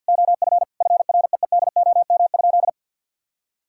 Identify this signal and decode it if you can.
Morse code, heard as a keyed tone, about 50 wpm, 700 Hz. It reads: OF FREEDOM?
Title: Sentences from Top 1600 Words 50wpm